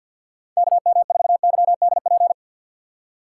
Morse code, heard as a keyed tone, about 53 wpm, 700 Hz. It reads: K M 4 Y D P
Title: USA Call Signs - 6 Characters - 6x Character Spacing - 50wpm